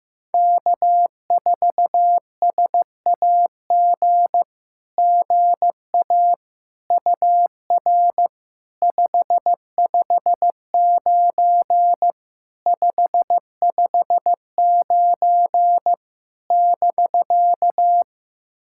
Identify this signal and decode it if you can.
Morse code, heard as a keyed tone, about 15 wpm, 700 Hz. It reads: K4SAG GA UR 559 559 <BK>
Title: SOTA QSO Practice 15wpm